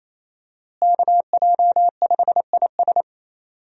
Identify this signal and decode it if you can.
Morse code, heard as a keyed tone, about 28 wpm, 700 Hz. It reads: KJ5IS
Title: USA Call Signs - 5 Characters - Repeated 2x - 28wpm